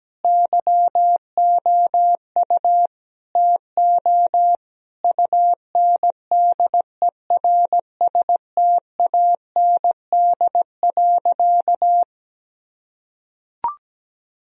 Morse code, heard as a keyed tone, about 17 wpm, 700 Hz. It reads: YOU TO UNDERSTAND.
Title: Sentences from Top 1000 Words - Review Part 3 - 17wpm